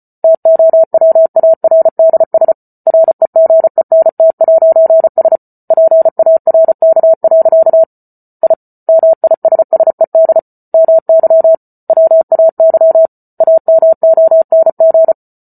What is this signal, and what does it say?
Morse code, heard as a keyed tone, about 34 wpm, 650 Hz. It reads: TOWARDS REGENT'S PARK. I MISSED MY WAY AMONG